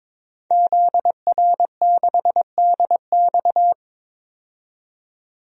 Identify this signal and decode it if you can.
Morse code, heard as a keyed tone, about 22 wpm, 700 Hz. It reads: ZR6DX
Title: DXCC-100 Call Signs - 5 Characters - 22wpm